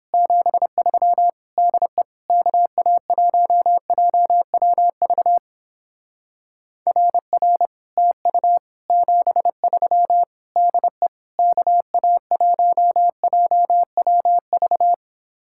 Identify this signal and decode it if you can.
Morse code, heard as a keyed tone, about 29 wpm, 700 Hz. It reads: 73 DE KA1JWV RR TU 73 DE KA1JWV